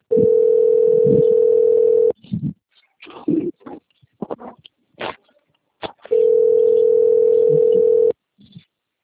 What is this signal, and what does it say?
Ringback tone (North America), 440 + 480 Hz.